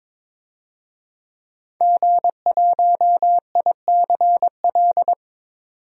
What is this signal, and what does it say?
Morse code, heard as a keyed tone, about 22 wpm, 700 Hz. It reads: G1ICL
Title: DXCC-100 Call Signs - 5 Characters - 22wpm